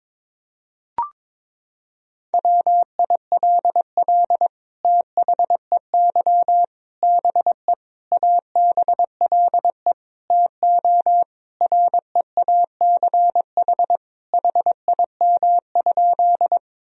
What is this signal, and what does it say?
Morse code, heard as a keyed tone, about 22 wpm, 700 Hz. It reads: WILL THEY BE ABLE TO REACH HIM?
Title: Sentences from Top 1000 Words - Review Part 2 - 22wpm